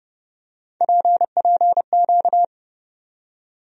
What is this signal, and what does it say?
Morse code, heard as a keyed tone, about 29 wpm, 700 Hz. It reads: PPQ
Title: USA Call Sign Postfixes - 3 Letters 30wpm